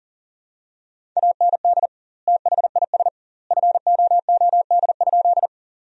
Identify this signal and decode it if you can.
Morse code, heard as a keyed tone, about 40 wpm, 700 Hz. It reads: AND THIS FOOD?
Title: 2x Word-Spacing - Sentences from Top 300 Words - 40wpm